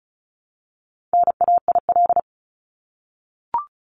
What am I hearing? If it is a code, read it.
Morse code, heard as a keyed tone, about 34 wpm, 700 Hz. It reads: NAIL
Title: Top 1501-1750 Words 35wpm